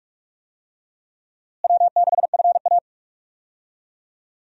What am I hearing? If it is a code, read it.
Morse code, heard as a keyed tone, about 42 wpm, 700 Hz. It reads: W6FA
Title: USA Call Signs - 4 Characters - Repeated 3x - 45wpm